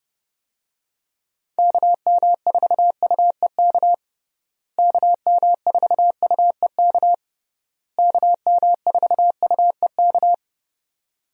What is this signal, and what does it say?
Morse code, heard as a keyed tone, about 30 wpm, 700 Hz. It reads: KM4UEK KM4UEK KM4UEK
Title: USA Call Signs - 6 Characters - Repeated 3x - 30wpm